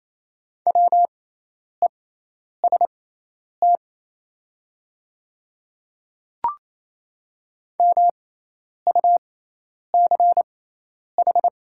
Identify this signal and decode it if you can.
Morse code, heard as a keyed tone, about 28 wpm, spaced as for 10 wpm, 700 Hz. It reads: WEST MUCH
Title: 6x Character Spacing - Top 500 Words - 28wpm